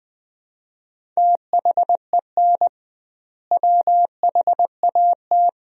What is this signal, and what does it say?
Morse code, heard as a keyed tone, about 20 wpm, 700 Hz. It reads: THEN WHAT
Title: 2x Word-Spacing - Sentences from Top 400 Words - 20wpm